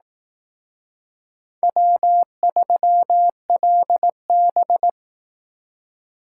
Morse code, heard as a keyed tone, about 18 wpm, 700 Hz. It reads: W3LB
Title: Speed Racing - USA 1x2 Call Signs 15wpm